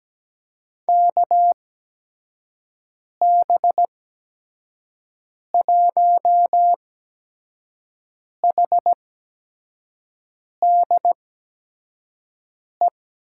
Morse code, heard as a keyed tone, about 17 wpm, spaced as for 5 wpm, 700 Hz. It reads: KB1HDE
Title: USA Call Signs - 6 Characters - 8x Character Spacing - 17wpm